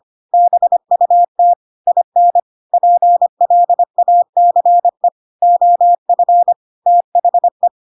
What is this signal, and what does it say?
Morse code, heard as a keyed tone, about 25 wpm, 700 Hz. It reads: BUT IN PLACE OF THE
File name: Pinocchio / Pinocchio ch5 25 25 ik1hge